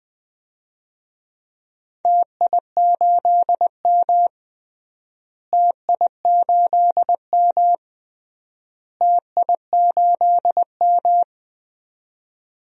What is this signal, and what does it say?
Morse code, heard as a keyed tone, about 20 wpm, 700 Hz. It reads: TI8M TI8M TI8M
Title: DXCC-100 Call Signs - 4 Characters - Repeated 3x - 20wpm